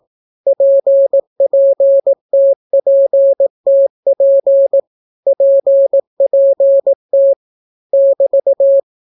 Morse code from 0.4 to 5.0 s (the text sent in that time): PPTPTP